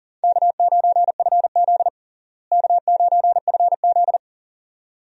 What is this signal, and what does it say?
Morse code, heard as a keyed tone, about 40 wpm, 700 Hz. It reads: K9FZ K9FZ
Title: USA Call Signs - 4 Characters - Repeated 2x - 40wpm